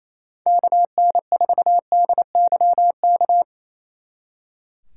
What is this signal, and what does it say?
Morse code, heard as a keyed tone, about 36 wpm, 700 Hz. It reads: KN4DYK